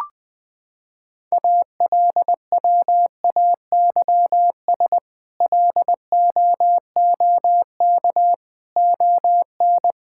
Morse code, heard as a keyed tone, about 20 wpm, 700 Hz. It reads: ALWAYS LOOK ON
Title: Sets of 8 Words - 20wpm